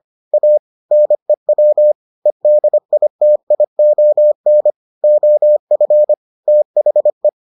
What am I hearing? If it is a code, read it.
Morse code, heard as a keyed tone, about 25 wpm, 600 Hz. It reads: A NEW EDITION OF THE